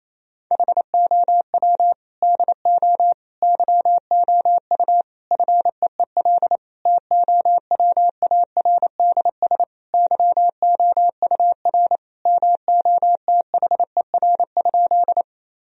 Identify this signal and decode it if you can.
Morse code, heard as a keyed tone, about 28 wpm, 700 Hz. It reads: HOW DO YOU FEEL TOWARDS YOUR MOTHER?
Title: Sentences from Top 1000 Words - Review Part 2 - 28wpm